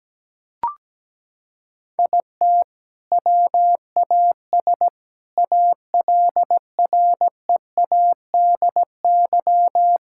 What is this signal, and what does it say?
Morse code, heard as a keyed tone, about 17 wpm, 700 Hz. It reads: IT WAS ALREADY
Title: Sentences from Top 1000 Words - Review Part 5 - 17wpm